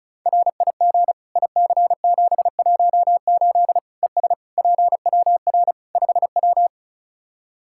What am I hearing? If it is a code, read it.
Morse code, heard as a keyed tone, about 36 wpm, 700 Hz. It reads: RIG IC718 ES PWR 5W
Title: K1USN SST QSO Practice 35wpm